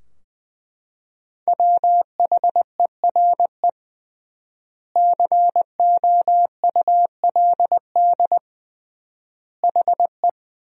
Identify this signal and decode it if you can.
Morse code, heard as a keyed tone, about 20 wpm, 700 Hz. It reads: WHERE COULD HE
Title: 3x Word-Spacing - Sentences from Top 200 Words - 20wpm